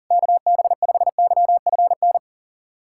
Morse code, heard as a keyed tone, about 40 wpm, 700 Hz. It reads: KB5YFN
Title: SOTA QSO Practice 40wpm